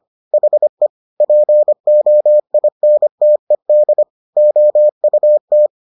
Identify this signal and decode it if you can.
Morse code, heard as a keyed tone, about 25 wpm, 600 Hz. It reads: HE POINTED OUT